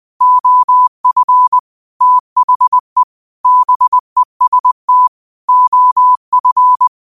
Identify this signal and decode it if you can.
Morse code, heard as a keyed tone, about 20 wpm, 1000 Hz. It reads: OF THE BEST OF
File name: qotdfiles / 20wpm